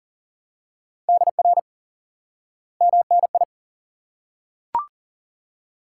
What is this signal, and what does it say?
Morse code, heard as a keyed tone, about 37 wpm, 700 Hz. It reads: DR MNI
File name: IQE Course 3 - Lesson 89 - YAGI - Two Elements - 40wpm